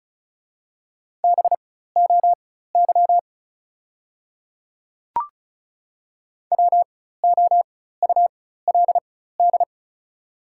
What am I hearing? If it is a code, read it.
Morse code, heard as a keyed tone, about 36 wpm, spaced as for 16 wpm, 700 Hz. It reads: BOY WOULD
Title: 4x Character Spacing - Top 300 Words - 35wpm